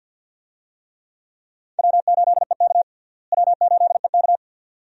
Morse code, heard as a keyed tone, about 52 wpm, 700 Hz. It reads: W8EK W8EK
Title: USA Call Signs - 4 Characters - Repeated 2x - 50wpm